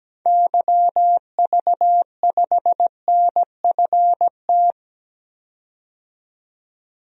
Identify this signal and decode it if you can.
Morse code, heard as a keyed tone, about 17 wpm, 700 Hz. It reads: YV5NFT
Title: DXCC-100 Call Signs - 6 Characters - 17wpm